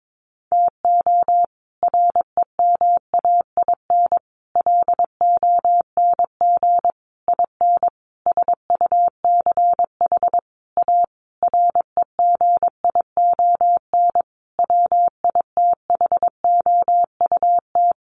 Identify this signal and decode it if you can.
Morse code, heard as a keyed tone, about 22 wpm, 700 Hz. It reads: TO REMAIN LONG IN SUCH A REGION WITHOUT
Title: Sentences from Top 1500 Words 22wpm